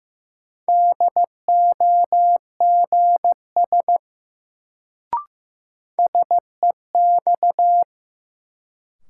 Morse code, heard as a keyed tone, about 15 wpm, 700 Hz. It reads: DOGS SEX
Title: Sentences from Top 1600 Words - New Word Drill Alt - 15wpm